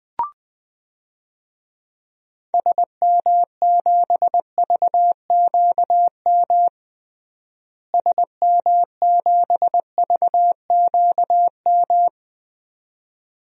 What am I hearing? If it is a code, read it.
Morse code, heard as a keyed tone, about 20 wpm, 700 Hz. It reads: SM7VQM SM7VQM
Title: DXCC-100 Call Signs - 6 Characters - Repeated 2x - 20wpm